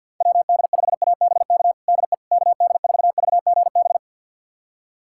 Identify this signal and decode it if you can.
Morse code, heard as a keyed tone, about 50 wpm, 700 Hz. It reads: WD5ABK DE KD4VGB
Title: USA Callsign DE Callsign 50wpm